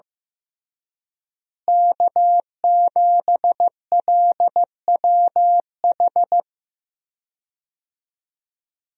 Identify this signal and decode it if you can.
Morse code, heard as a keyed tone, about 16 wpm, 700 Hz. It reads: K7LWH
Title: USA Call Signs - 5 Characters - Repeated 3x - 15wpm